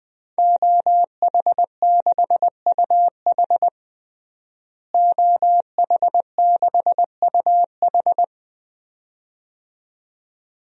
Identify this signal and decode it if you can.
Morse code, heard as a keyed tone, about 20 wpm, 700 Hz. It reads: OH6UH OH6UH
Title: DXCC-100 Call Signs - 5 Characters - Repeated 2x - 20wpm